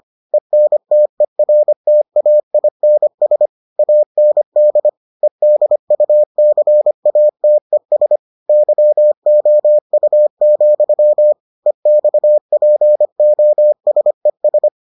Morse code, heard as a keyed tone, about 25 wpm, 600 Hz. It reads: ENTERTAINS AND EDUCATES YOU, EXPOSES